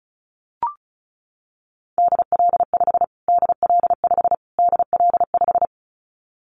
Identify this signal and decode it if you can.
Morse code, heard as a keyed tone, about 34 wpm, 700 Hz. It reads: DL5 DL5 DL5
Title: Callsign Prefixes for Top 25 DXCC - Repeated 3x - 35wpm